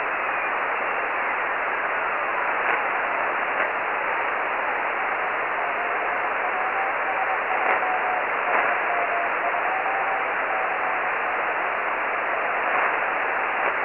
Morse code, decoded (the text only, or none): none